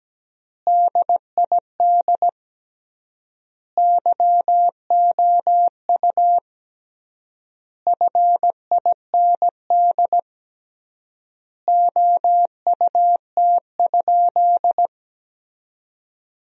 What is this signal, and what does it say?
Morse code, heard as a keyed tone, about 17 wpm, 700 Hz. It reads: DID YOU FIND OUT?
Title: 3x Word-Spacing - Sentences from Top 100 Words - 17wpm